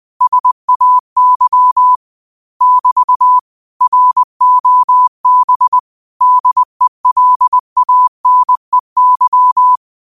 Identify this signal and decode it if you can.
Morse code, heard as a keyed tone, about 20 wpm, 1000 Hz. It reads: SAY <BT> ROB DELANEY
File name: 20wpm